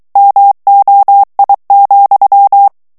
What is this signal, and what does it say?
Morse code, heard as a keyed tone, about 23 wpm, 800 Hz. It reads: MOI,
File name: morse mp3 / FRANCAIS TEXTE CLAIR 2 VITESSE 22 MOTS MINUTES